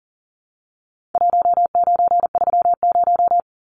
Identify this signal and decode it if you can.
Morse code, heard as a keyed tone, about 40 wpm, 700 Hz. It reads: 1930
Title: Numbers - Contest Style - 4-Digit 40wpm